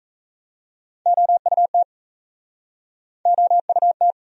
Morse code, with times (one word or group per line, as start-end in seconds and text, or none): 1.1-1.8 OUT
3.3-4.1 OUT